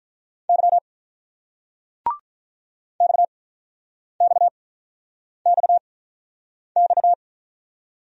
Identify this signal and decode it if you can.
Morse code, heard as a keyed tone, about 45 wpm, spaced as for 9 wpm, 700 Hz. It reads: Y XXXX